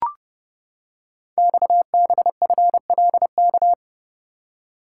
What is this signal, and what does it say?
Morse code, heard as a keyed tone, about 30 wpm, 700 Hz. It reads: XBFLK
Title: Code Groups - 5 Letters - 30wpm